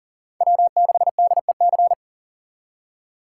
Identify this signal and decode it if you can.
Morse code, heard as a keyed tone, about 40 wpm, 700 Hz. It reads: W6DEC
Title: USA Call Signs - 5 Characters - 3x Character Spacing - 40wpm